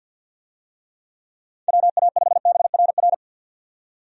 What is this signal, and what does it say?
Morse code, heard as a keyed tone, about 51 wpm, 700 Hz. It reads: WA5BRR
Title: USA Call Signs - 6 Characters - 50wpm